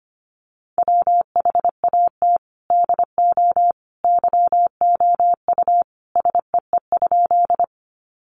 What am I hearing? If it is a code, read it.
Morse code, heard as a keyed tone, about 25 wpm, 700 Hz. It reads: WHAT DO YOU SEE?
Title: Sentences from Top 100 Words 25wpm